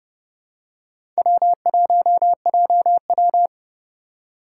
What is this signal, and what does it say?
Morse code, heard as a keyed tone, about 31 wpm, 700 Hz. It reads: W1JW
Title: USA Call Signs - 4 Characters - 6x Character Spacing - 30wpm